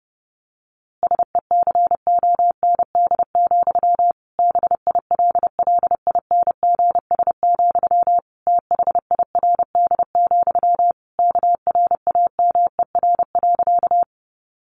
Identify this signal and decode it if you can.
Morse code, heard as a keyed tone, about 30 wpm, 700 Hz. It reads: SECOND, BILLINGS, THIRD, KRAMER.
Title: A Sound Of Thunder 30wpm